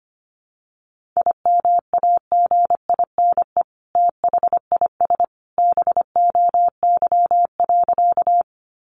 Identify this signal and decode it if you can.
Morse code, heard as a keyed tone, about 25 wpm, 700 Hz. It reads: IMAGINE THIS BOY.
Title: Sentences from Top 1400 Words 25wpm